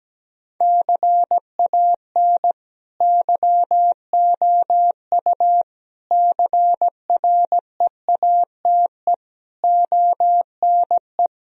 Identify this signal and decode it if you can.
Morse code, heard as a keyed tone, about 17 wpm, 700 Hz. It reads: CAN YOU CREATE ONE
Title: Sentences from Top 1000 Words - Review Part 3 - 17wpm